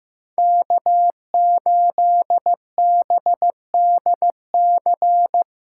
Morse code, heard as a keyed tone, about 15 wpm, 700 Hz. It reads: K8BDC
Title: USA Call Signs - 5 Characters - 4x Character Spacing - 15wpm